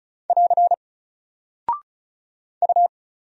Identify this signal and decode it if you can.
Morse code, heard as a keyed tone, about 36 wpm, 700 Hz. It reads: <AR> U
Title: Single Character (Letter:Number:Punctuation:Prosign) 35wpm